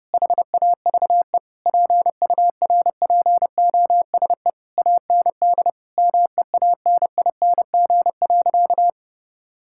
Morse code, heard as a keyed tone, about 30 wpm, 700 Hz. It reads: HAVE PURPOSE AND MEANING.
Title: Sentences from Top 1000 Words - Review Part 6 - 30wpm